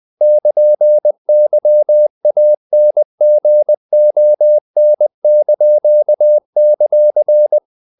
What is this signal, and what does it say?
Morse code, heard as a keyed tone, about 20 wpm, 600 Hz. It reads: <KN>YANGON);
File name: CW / Practice / NationalCapitals20